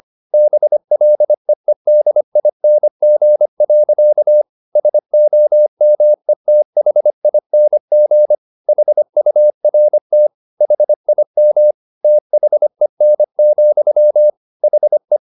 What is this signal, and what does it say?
Morse code, heard as a keyed tone, about 25 wpm, 600 Hz. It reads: BLEEDING. SOMETHING HURT HIM THEN, HE
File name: the old man and the sea part 1 25wpm